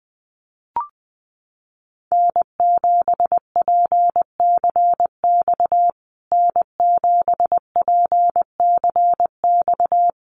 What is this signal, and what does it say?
Morse code, heard as a keyed tone, about 20 wpm, 700 Hz. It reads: N7PCX N7PCX
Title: USA Call Signs Repeated 3x 20wpm